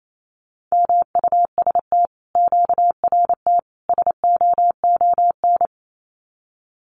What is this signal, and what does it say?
Morse code, heard as a keyed tone, about 28 wpm, 700 Hz. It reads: MUST QRT SOON
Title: Common QSO Phrases Encore 28wpm